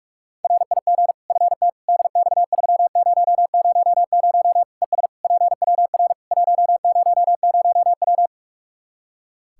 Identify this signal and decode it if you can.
Morse code, heard as a keyed tone, about 47 wpm, 700 Hz. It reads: RIG FT DX3000 ES PWR 100W
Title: Practice QSO - Exchange 2 - Info - 45wpm